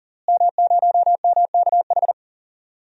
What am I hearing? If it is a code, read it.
Morse code, heard as a keyed tone, about 40 wpm, 700 Hz. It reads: M0MKH
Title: DXCC-100 Call Signs - 5 Characters - Repeated 2x - 40wpm